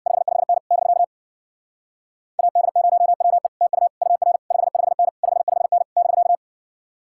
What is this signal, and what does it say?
Morse code, heard as a keyed tone, about 62 wpm, 700 Hz. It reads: 55N <BK> AD0WE TU UR 55N 55N <BK>
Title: Speed-Racing Common QSO Phrases - 50wpm